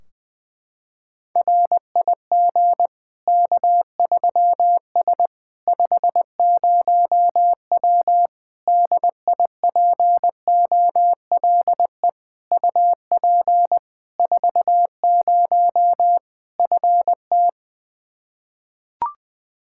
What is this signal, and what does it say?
Morse code, heard as a keyed tone, about 20 wpm, 700 Hz. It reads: RIG K3S 50W DIPOLE UP 40 FT
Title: K1USN SST QSO Practice 20wpm